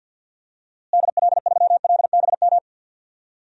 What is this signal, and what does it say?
Morse code, heard as a keyed tone, about 52 wpm, 700 Hz. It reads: DL3LBM
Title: DXCC-100 Call Signs - 6 Characters - Repeated 3x - 50wpm